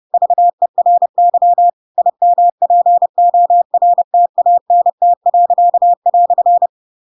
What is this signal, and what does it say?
Morse code, heard as a keyed tone, about 30 wpm, 700 Hz. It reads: VERY IMPORTANT."
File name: Pinocchio / Pinocchio ch8 30 30 ik1hge